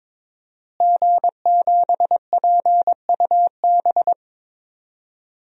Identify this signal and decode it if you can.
Morse code, heard as a keyed tone, about 21 wpm, 700 Hz. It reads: G7PUB
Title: DXCC-100 Call Signs - 5 Characters - Repeated 3x - 22wpm